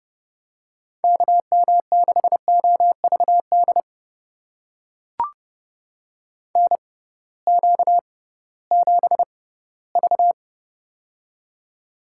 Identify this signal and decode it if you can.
Morse code, heard as a keyed tone, about 31 wpm, spaced as for 18 wpm, 700 Hz. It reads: KM6OVD NQ7V
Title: USA Call Signs - 6x Character Spacing - 30wpm